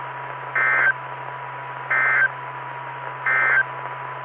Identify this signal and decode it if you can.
Emergency Alert System (SAME) tones, heard as harsh buzzing bursts. It is heard from 0.5 to 3.7 s.
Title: Saline, AK SKYWARN Repeater System - W5RHS